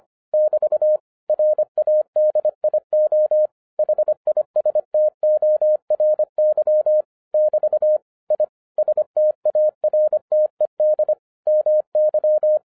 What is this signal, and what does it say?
Morse code, heard as a keyed tone, about 25 wpm, 610 Hz. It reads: <BT> RADIO HISTORY <BT> I STARTED MY